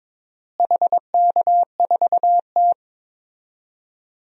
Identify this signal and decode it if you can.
Morse code, heard as a keyed tone, about 22 wpm, 700 Hz. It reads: HK4T
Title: DXCC-100 Call Signs - 4 Characters - Repeated 2x - 22wpm